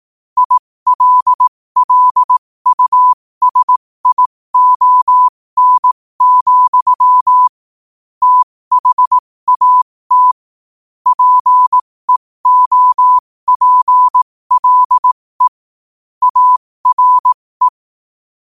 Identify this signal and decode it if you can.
Morse code, heard as a keyed tone, about 18 wpm, spaced as for 15 wpm, 1000 Hz. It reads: ILLUSION, THAT PEOPLE ARE
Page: qotd-at-15-wpm